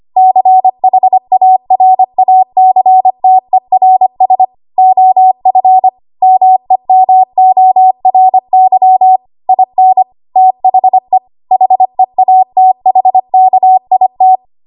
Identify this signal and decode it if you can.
Morse code, heard as a keyed tone, about 25 wpm, 750 Hz. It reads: CHARACTERS OF MEMORY IN THE HEATHKIT
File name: Archive / 25 WPM / 25WPM